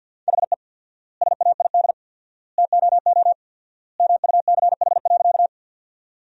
Morse code, heard as a keyed tone, about 50 wpm, 700 Hz. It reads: HE SAID TOO MUCH.
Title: 4x Word-Spacing - Sentences from Top 200 Words - 50wpm